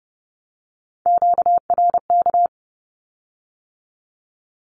Morse code, heard as a keyed tone, about 31 wpm, 700 Hz. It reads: QRK
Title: Q-Codes with Spoken Meaning - 30wpm